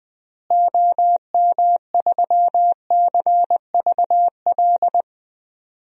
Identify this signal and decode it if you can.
Morse code, heard as a keyed tone, about 21 wpm, 700 Hz. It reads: OM3CVL
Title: DXCC-100 Call Signs - 6 Characters - Repeated 3x - 20wpm